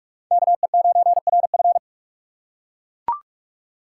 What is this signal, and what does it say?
Morse code, heard as a keyed tone, about 42 wpm, 690 Hz. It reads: KE9RF
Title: USA Call Signs - 5 Characters - 6x Character Spacing - 45wpm